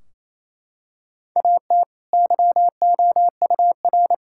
Morse code, heard as a keyed tone, about 28 wpm, 700 Hz. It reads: AT YOUR
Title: Sentences from Top 1600 Words 28wpm